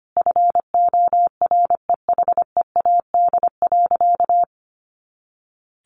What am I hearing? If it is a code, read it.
Morse code, heard as a keyed tone, about 25 wpm, 700 Hz. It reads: FOREHEAD.
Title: Easy Phrases 25wpm